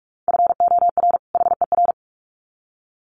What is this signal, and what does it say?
Morse code, heard as a keyed tone, about 44 wpm, 700 Hz. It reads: FOR HER
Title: Sets of 2 Words 45wpm